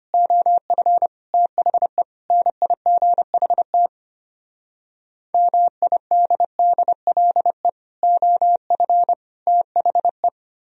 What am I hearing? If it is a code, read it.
Morse code, heard as a keyed tone, about 32 wpm, 700 Hz. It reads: OF THE NIGHT MIDDLE OF THE